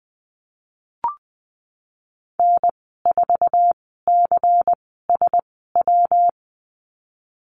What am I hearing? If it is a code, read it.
Morse code, heard as a keyed tone, about 20 wpm, spaced as for 15 wpm, 700 Hz. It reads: N4CSW
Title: 161 Lesson 26g - Letter g - Call Signs 20wpm